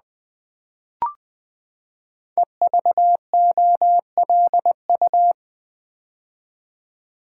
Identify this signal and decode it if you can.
Morse code, heard as a keyed tone, about 20 wpm, 700 Hz. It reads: EVOLU
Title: Build A Word - 5 Letters At A Time - 12-Letter Words - 20wpm